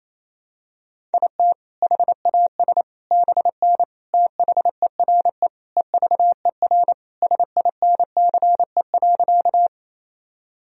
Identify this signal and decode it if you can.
Morse code, heard as a keyed tone, about 28 wpm, 700 Hz. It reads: IT HAS BN THERE EVER SINCE.
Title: Sentences from Top 400 Words and Abbreviations 28wpm